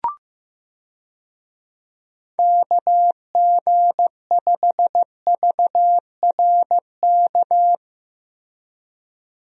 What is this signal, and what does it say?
Morse code, heard as a keyed tone, about 15 wpm, 700 Hz. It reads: KG5VRK
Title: USA Call Signs - 6 Characters - Repeated 2x - 15wpm